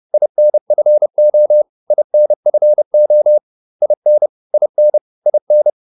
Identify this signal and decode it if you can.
Morse code, heard as a keyed tone, about 30 wpm, 600 Hz. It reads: INFO INFO IN IN IN